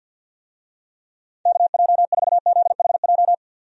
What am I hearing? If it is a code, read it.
Morse code, heard as a keyed tone, about 53 wpm, 700 Hz. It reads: KJ4ZHJ
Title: USA Call Signs - 6 Characters - 50wpm